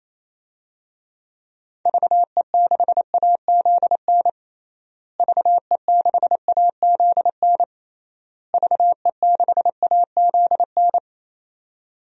Morse code, heard as a keyed tone, about 28 wpm, 700 Hz. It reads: VE6AZN VE6AZN VE6AZN
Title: DXCC-100 Call Signs - 6 Characters - Repeated 3x - 28wpm